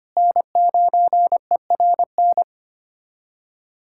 Morse code, heard as a keyed tone, about 25 wpm, 700 Hz. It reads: N9ERN
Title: USA Callsign DE Callsign - 5 Characters - 25wpm